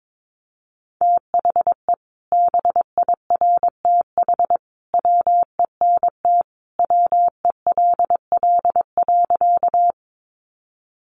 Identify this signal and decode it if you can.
Morse code, heard as a keyed tone, about 22 wpm, 700 Hz. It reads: THE BIRTH WENT WELL.
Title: Sentences from Top 800 Words 22wpm